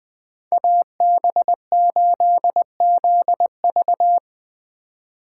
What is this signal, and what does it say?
Morse code, heard as a keyed tone, about 20 wpm, 700 Hz. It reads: AB8ZV
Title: Speed Racing - USA 2x2 Call Signs 17wpm